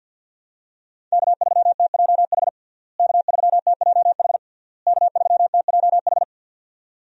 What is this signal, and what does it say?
Morse code, heard as a keyed tone, about 51 wpm, 700 Hz. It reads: K3TJH K3TJH K3TJH
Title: USA Call Signs - 5 Characters - Repeated 3x - 50wpm